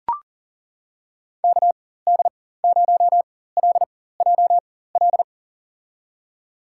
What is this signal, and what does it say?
Morse code, heard as a keyed tone, about 40 wpm, spaced as for 19 wpm, 700 Hz. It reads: KD0LJL
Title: USA Call Signs - 6 Characters - 4x Character Spacing - 40wpm